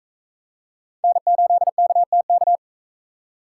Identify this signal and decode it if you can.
Morse code, heard as a keyed tone, about 42 wpm, 700 Hz. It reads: N8KTK